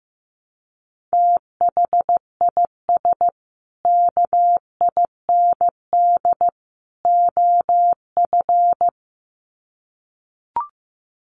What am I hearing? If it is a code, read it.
Morse code, heard as a keyed tone, about 15 wpm, 700 Hz. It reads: THIS KIND OF
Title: Sets of 3 Words 15wpm